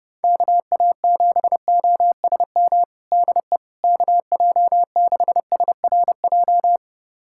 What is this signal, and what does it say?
Morse code, heard as a keyed tone, about 32 wpm, 700 Hz. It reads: KA7OSM DE KJ6SRJ
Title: USA Callsign DE Callsign 30wpm